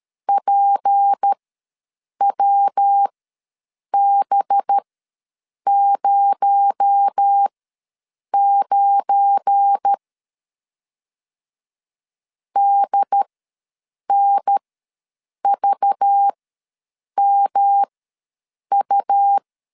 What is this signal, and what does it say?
Morse code, heard as a keyed tone, about 13 wpm, spaced as for 7 wpm, 800 Hz. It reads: PWB09 DNVMU
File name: LEZIONI / 9 286 ws 12wpm 3351